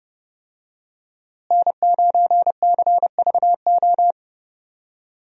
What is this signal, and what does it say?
Morse code, heard as a keyed tone, about 32 wpm, 700 Hz. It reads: N9CVO
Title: USA Call Signs - 5 Characters - Repeated 3x - 30wpm